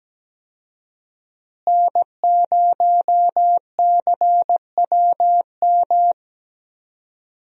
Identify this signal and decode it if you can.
Morse code, heard as a keyed tone, about 17 wpm, 700 Hz. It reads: N0CWM
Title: USA Call Signs - 5 Characters - Repeated 3x - 17wpm